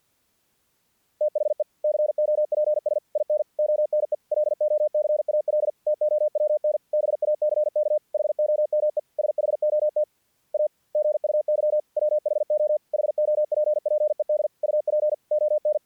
Morse code, heard as a keyed tone, about 50 wpm, 600 Hz. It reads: THE COPS IN ONE LOCAL TOWN BACK HOME SHOT A GUY WHO HOPPED UP ON
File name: archived / CRSnet